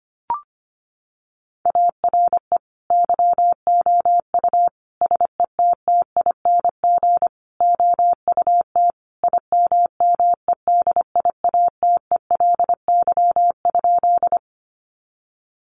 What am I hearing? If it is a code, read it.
Morse code, heard as a keyed tone, about 25 wpm, 700 Hz. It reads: ARE YOU SETTING OUT IMMEDIATELY?
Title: Sentences from Top 1300 Words 25wpm